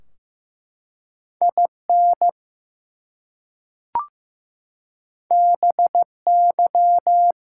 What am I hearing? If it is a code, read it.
Morse code, heard as a keyed tone, about 15 wpm, 700 Hz. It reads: IN BY
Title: Two Letter Words - 15wpm